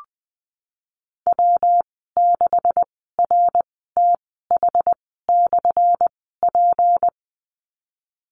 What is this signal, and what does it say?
Morse code, heard as a keyed tone, about 20 wpm, spaced as for 15 wpm, 700 Hz. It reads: W6RTH/P
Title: 179 Lesson 29g - Slash - Call Signs 20wpm